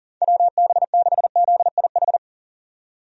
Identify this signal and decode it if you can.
Morse code, heard as a keyed tone, about 40 wpm, 700 Hz. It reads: WB6ZIH
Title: USA Call Signs - 6 Characters - 40wpm